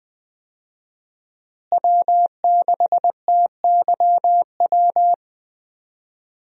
Morse code, heard as a keyed tone, about 20 wpm, 700 Hz. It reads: W6TYW
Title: USA Call Signs - 5 Characters - Repeated 2x - 20wpm